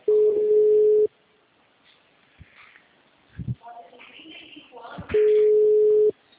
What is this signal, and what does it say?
Ringback tone (Europe, 425 Hz), with speech.
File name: out- -2035- - - .75251